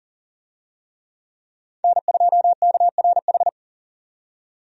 Morse code, heard as a keyed tone, about 40 wpm, 700 Hz. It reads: N2KRH
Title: USA Call Signs - 5 Characters - 40wpm